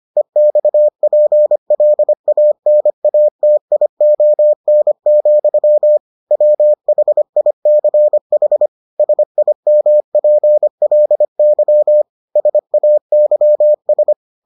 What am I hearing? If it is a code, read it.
Morse code, heard as a keyed tone, about 25 wpm, 600 Hz. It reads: EXPLANATION, WHICH SIMPLY SAYS